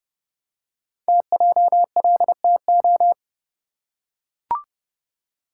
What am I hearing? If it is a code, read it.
Morse code, heard as a keyed tone, about 29 wpm, 700 Hz. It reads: TJLTO
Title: Code Groups - 5 Letters - 30wpm